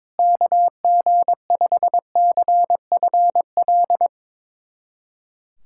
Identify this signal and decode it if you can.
Morse code, heard as a keyed tone, about 22 wpm, 700 Hz. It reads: KG5CFL
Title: USA Callsign DE Callsign - 6 Characters - 22wpm